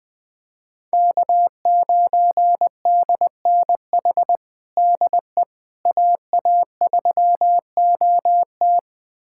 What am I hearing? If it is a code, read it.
Morse code, heard as a keyed tone, about 21 wpm, 700 Hz. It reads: K9DNH DE AA3OT
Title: USA Callsign DE Callsign - 5 Characters - 20wpm